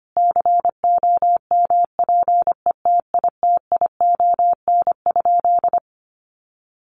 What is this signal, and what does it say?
Morse code, heard as a keyed tone, about 25 wpm, 700 Hz. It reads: COMPETITION?
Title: Sentences from Top 1000 Words 25wpm